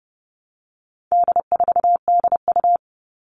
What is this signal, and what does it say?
Morse code, heard as a keyed tone, about 30 wpm, 700 Hz. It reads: D4DU
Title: All Cut Numbers - 4-Digit 30wpm